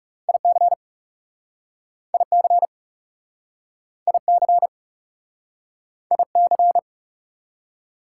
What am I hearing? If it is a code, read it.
Morse code, heard as a keyed tone, about 37 wpm, 690 Hz. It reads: IC IC IC IC
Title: Speed-Racing Prefixes and Suffixes 30wpm